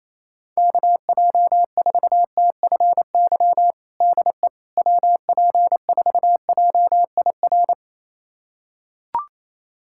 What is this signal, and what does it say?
Morse code, heard as a keyed tone, about 27 wpm, 700 Hz. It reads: KJ4TFY DE WP4JIR
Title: USA Callsign DE Callsign 28wpm